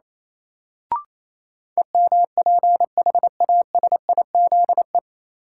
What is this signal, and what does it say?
Morse code, heard as a keyed tone, about 28 wpm, 700 Hz. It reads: EMPHASIZE
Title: Sentences from Top 1500 Words - New Word Drill Alt - 28wpm